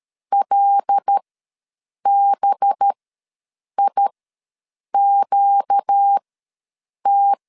Morse code, heard as a keyed tone, about 13 wpm, spaced as for 7 wpm, 800 Hz. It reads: LBIQT
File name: LEZIONI / 5 LBVC / 5 LB ws 12wpm 3034